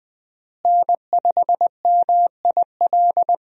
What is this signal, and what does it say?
Morse code, heard as a keyed tone, about 20 wpm, 700 Hz. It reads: N5MIL
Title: USA Call Signs - 5 Characters - Repeated 3x - 20wpm